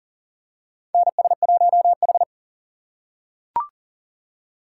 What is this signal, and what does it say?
Morse code, heard as a keyed tone, about 40 wpm, 700 Hz. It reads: NS1H
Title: USA Call Signs - 4 Characters - 8x Character Spacing - 40wpm